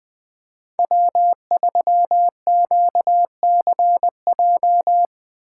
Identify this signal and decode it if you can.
Morse code, heard as a keyed tone, about 20 wpm, 700 Hz. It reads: W3QCJ
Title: USA Call Signs - 5 Characters - 4x Character Spacing - 20wpm